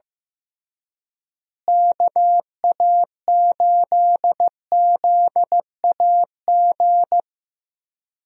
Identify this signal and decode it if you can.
Morse code, heard as a keyed tone, about 15 wpm, 700 Hz. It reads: KA8ZAG
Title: USA Call Signs - 6 Characters - Repeated 2x - 15wpm